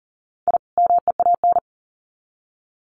Morse code, heard as a keyed tone, about 40 wpm, 700 Hz. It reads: I MEAN
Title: Sets of 2 Words 40wpm